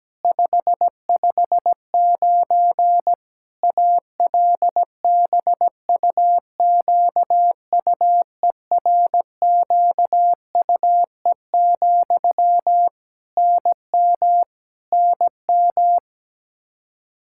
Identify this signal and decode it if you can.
Morse code, heard as a keyed tone, about 17 wpm, 700 Hz. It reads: 559 ALBUQUERQUE, NM NM
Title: Practice QSO - Exchange 1 - Beginning - 17wpm